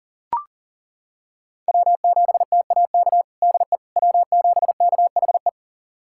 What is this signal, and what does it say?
Morse code, heard as a keyed tone, about 40 wpm, 700 Hz. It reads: W7TAK DE W7KHE
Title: USA Callsign DE Callsign - 5 Characters - 40wpm